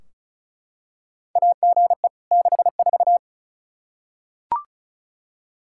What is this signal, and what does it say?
Morse code, heard as a keyed tone, about 36 wpm, 700 Hz. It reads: AGE 64
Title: Practice QSO - Exchange 3 - More Info - 35wpm